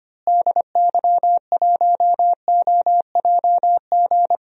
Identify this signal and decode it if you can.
Morse code, heard as a keyed tone, about 25 wpm, 700 Hz. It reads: DY1OJG
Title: DXCC-100 Call Signs - 6 Characters - Repeated 2x - 25wpm